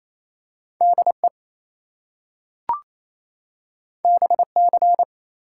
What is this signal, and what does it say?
Morse code, heard as a keyed tone, about 28 wpm, 700 Hz. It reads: DE BC
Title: US States and Canadian Provinces 28wpm